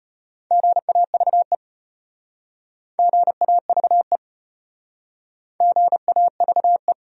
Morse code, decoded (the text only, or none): GAVE GAVE GAVE